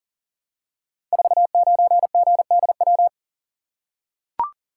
Morse code, heard as a keyed tone, about 41 wpm, 700 Hz. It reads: 49GDW
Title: Code Groups - 5 Letter:Numbers - 40wpm